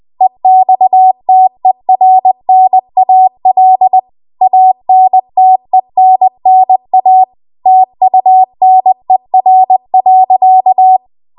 Morse code, heard as a keyed tone, about 20 wpm, 750 Hz. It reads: EXTERNAL ANTENNA TUNER.
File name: Archive / 20 WPM / 20WPM